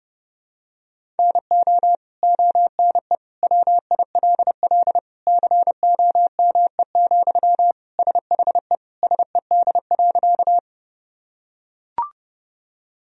Morse code, heard as a keyed tone, about 32 wpm, 700 Hz. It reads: NO ONE WILL COME, SHE SED.
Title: Sentences from Top 100 Words and Abbreviations 30wpm